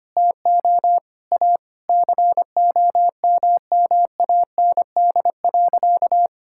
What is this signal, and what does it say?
Morse code, heard as a keyed tone, about 25 wpm, 700 Hz. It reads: TO A COMMAND.
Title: Sentences from Top 1600 Words 25wpm